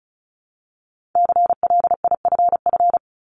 Morse code, heard as a keyed tone, about 36 wpm, 700 Hz. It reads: CLIFF
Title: Multi-letter Phonemes Repeated 3x 35wpm